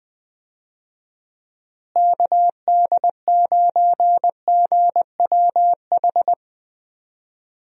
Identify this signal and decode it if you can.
Morse code, heard as a keyed tone, about 20 wpm, 700 Hz. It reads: KD9GWH
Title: USA Call Signs - 6 Characters - Repeated 2x - 20wpm